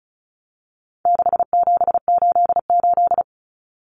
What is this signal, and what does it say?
Morse code, heard as a keyed tone, about 34 wpm, 700 Hz. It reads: 6788
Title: Numbers - 4-Digit 35wpm